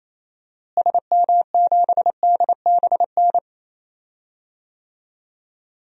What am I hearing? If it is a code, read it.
Morse code, heard as a keyed tone, about 27 wpm, 700 Hz. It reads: SM7DBN
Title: DXCC-100 Call Signs - 6 Characters - Repeated 3x - 28wpm